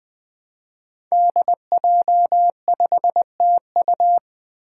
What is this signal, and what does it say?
Morse code, heard as a keyed tone, about 20 wpm, 700 Hz. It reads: DJ5TU
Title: Five Challenging Letter-Number Combinations 20wpm